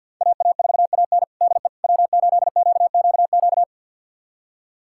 Morse code, heard as a keyed tone, about 49 wpm, 700 Hz. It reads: AA4AN DE W8QQQ
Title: USA Callsign DE Callsign 50wpm